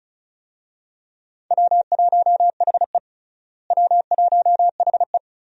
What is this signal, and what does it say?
Morse code, heard as a keyed tone, about 36 wpm, 700 Hz. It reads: W1HE W1HE
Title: USA Call Signs - 4 Characters - Repeated 3x - 35wpm